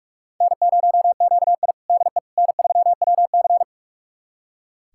Morse code, heard as a keyed tone, about 44 wpm, 700 Hz. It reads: N0QI DE N3WC
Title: USA Callsign DE Callsign - 4 Characters - 45wpm